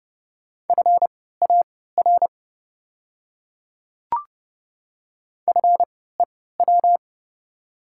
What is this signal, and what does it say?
Morse code, heard as a keyed tone, about 30 wpm, spaced as for 17 wpm, 700 Hz. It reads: FAR FEW
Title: 3x Character Spacing - Top 300 Words - 30wpm